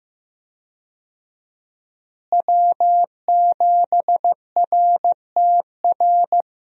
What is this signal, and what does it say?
Morse code, heard as a keyed tone, about 15 wpm, 700 Hz. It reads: W7RTR
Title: USA Call Signs - 5 Characters - Repeated 3x - 15wpm